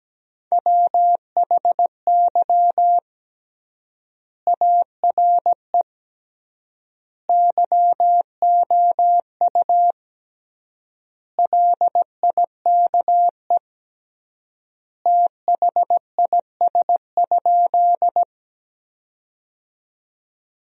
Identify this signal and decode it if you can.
Morse code, heard as a keyed tone, about 17 wpm, 700 Hz. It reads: WHY ARE YOU LIKE THIS?
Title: 3x Word-Spacing - Sentences from Top 200 Words - 17wpm